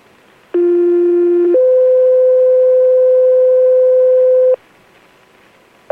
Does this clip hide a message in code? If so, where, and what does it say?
Yes: two-tone page from 0.5 to 4.5 s, heard as a one-second tone, then a three-second tone, about 350 then 510 Hz.